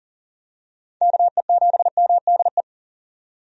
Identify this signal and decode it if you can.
Morse code, heard as a keyed tone, about 40 wpm, 700 Hz. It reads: KE7MDE
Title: USA Call Signs - 6 Characters - 40wpm